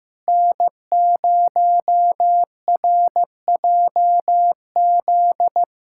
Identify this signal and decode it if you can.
Morse code, heard as a keyed tone, about 15 wpm, 700 Hz. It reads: N0RJZ
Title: USA Callsign DE Callsign - 5 Characters - 15wpm